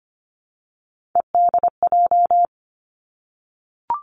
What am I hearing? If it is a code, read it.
Morse code, heard as a keyed tone, about 25 wpm, 700 Hz. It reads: EDJ
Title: Three Challenging Letter-Number Combinations 25wpm